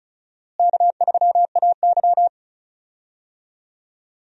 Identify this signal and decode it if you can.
Morse code, heard as a keyed tone, about 36 wpm, 700 Hz. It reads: K3AY
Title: USA Call Signs - 4 Characters - Repeated 2x - 35wpm